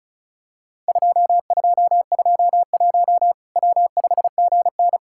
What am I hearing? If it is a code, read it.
Morse code, heard as a keyed tone, about 36 wpm, 700 Hz. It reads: 2221 W5GN
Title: Worked All Europe DX Contest - QTC Exchange - 35wpm